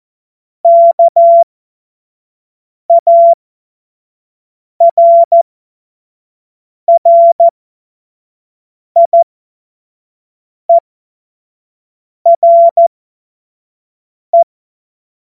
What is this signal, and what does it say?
Morse code, heard as a keyed tone, about 14 wpm, spaced as for 5 wpm, 680 Hz. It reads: KARRIERE
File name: Morsekurs / Witze / Witz 23. Mai 2023 25 F70